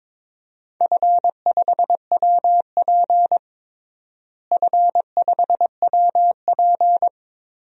Morse code, heard as a keyed tone, about 22 wpm, 700 Hz. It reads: F5WP F5WP
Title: DXCC-100 Call Signs - 4 Characters - Repeated 3x - 22wpm